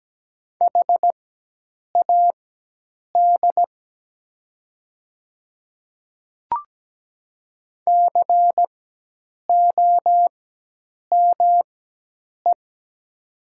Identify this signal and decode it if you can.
Morse code, heard as a keyed tone, about 17 wpm, spaced as for 8 wpm, 700 Hz. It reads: HAD COME
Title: 4x Character Spacing - Top 100 Words - 17wpm